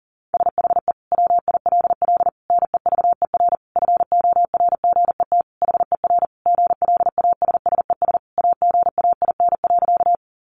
Morse code, heard as a keyed tone, about 40 wpm, 700 Hz. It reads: SHE WILL NEVER FORGET HER GLASSES AGAIN.
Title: Additional Easy Phrases 40wpm